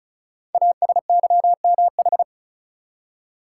Morse code, heard as a keyed tone, about 36 wpm, 690 Hz. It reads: ASYMH
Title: Code Groups - 5 Letters - 35wpm